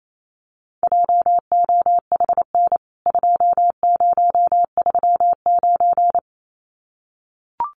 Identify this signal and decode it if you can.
Morse code, heard as a keyed tone, about 28 wpm, 700 Hz. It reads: JOHN 2039
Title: CWT Exchange 28wpm